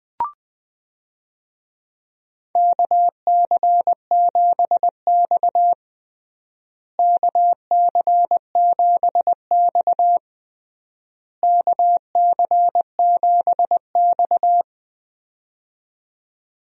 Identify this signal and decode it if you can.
Morse code, heard as a keyed tone, about 20 wpm, 700 Hz. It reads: KC7X KC7X KC7X
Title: USA Call Signs - 4 Characters - Repeated 3x - 20wpm